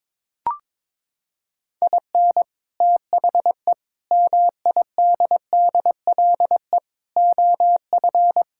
Morse code, heard as a keyed tone, about 21 wpm, 700 Hz. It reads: IN THE MIDDLE OF
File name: practice / 22wpm / Sentences with 4-Word Phrases - 22wpm